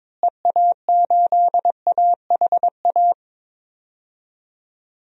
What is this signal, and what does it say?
Morse code, heard as a keyed tone, about 21 wpm, 700 Hz. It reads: EA8AHA
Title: DXCC-100 Call Signs - 6 Characters - Repeated 2x - 22wpm